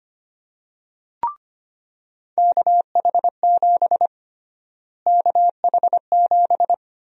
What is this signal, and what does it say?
Morse code, heard as a keyed tone, about 25 wpm, 700 Hz. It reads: KH7 KH7
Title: USA Call Sign Prefixes - Repeated 2x - 25wpm